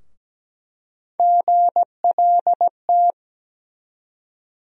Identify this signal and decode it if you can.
Morse code, heard as a keyed tone, about 17 wpm, 700 Hz. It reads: GLT
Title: USA Call Sign Postfixes - 3 Letters 17wpm